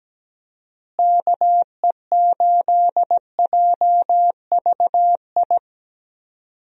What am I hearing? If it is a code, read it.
Morse code, heard as a keyed tone, about 17 wpm, 700 Hz. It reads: KE8JVI
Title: USA Call Signs - 6 Characters - Repeated 2x - 17wpm